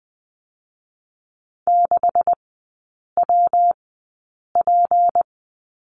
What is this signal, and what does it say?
Morse code, heard as a keyed tone, about 19 wpm, spaced as for 8 wpm, 700 Hz. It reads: 6WP
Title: 152 Lesson 25e - Letter p - Three Characters 20wpm